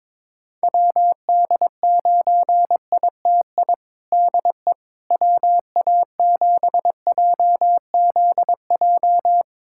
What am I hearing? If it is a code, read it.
Morse code, heard as a keyed tone, about 22 wpm, 700 Hz. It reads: WD9ITI DE WA7JZJ
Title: USA Callsign DE Callsign - 6 Characters - 22wpm